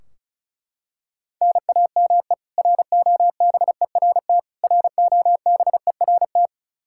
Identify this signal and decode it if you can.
Morse code, heard as a keyed tone, about 34 wpm, 700 Hz. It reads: NAME ROBERT ROBERT
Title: Practice QSO - Exchange 1 - Beginning - 35wpm